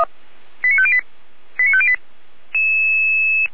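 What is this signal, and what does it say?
Five-tone selective call (ZVEI), heard as quick runs of five tones, tones of about 70 ms. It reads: 89508 89508
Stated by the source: ZVEI five-tone selective call received on SDR.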